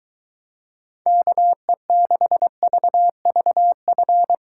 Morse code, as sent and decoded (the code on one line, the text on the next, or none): -.- . -.... ...- ...- ..-.
KE6VVF